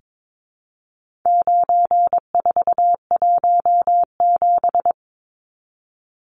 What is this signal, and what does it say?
Morse code, heard as a keyed tone, about 22 wpm, 700 Hz. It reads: 9417
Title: Numbers - 4-Digit 22wpm